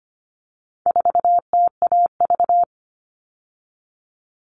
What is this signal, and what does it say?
Morse code, heard as a keyed tone, about 25 wpm, 700 Hz. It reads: 4TAV
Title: All Cut Numbers - 4-Digit 25wpm